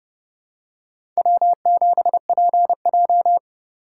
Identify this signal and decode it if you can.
Morse code, heard as a keyed tone, about 30 wpm, 700 Hz. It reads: W7PJ
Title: USA Call Signs - 3x Character Spacing - 30wpm